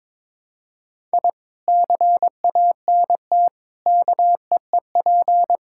Morse code, heard as a keyed tone, about 22 wpm, 700 Hz. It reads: I CANT KEEP
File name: practice / 22wpm / Sentences with 2-Word Phrases - 22wpm